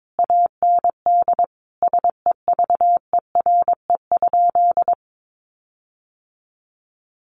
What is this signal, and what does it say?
Morse code, heard as a keyed tone, about 22 wpm, 700 Hz. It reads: AND SEVERE?
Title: Sentences from Top 1400 Words 22wpm